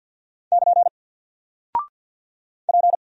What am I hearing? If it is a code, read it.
Morse code, heard as a keyed tone, about 48 wpm, 700 Hz. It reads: <KN> P
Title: Single Character (Letter:Number:Punctuation:Prosign) 50wpm